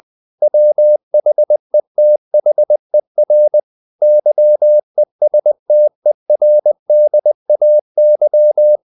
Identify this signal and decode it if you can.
Morse code, heard as a keyed tone, about 20 wpm, 600 Hz. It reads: WHETHER YESTERDAY
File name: the old man and the sea / the old man and the sea part 1 20wpm